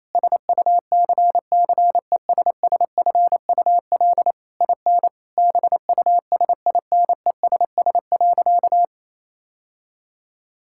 Morse code, heard as a keyed tone, about 28 wpm, 700 Hz. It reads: SUCCESSFUL IN BUSINESS.
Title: Sentences from Top 1000 Words - Review Part 6 - 28wpm